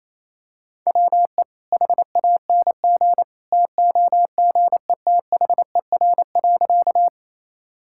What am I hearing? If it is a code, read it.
Morse code, heard as a keyed tone, about 28 wpm, 700 Hz. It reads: WE HANG TOGETHER.
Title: Sentences from Top 1600 Words 28wpm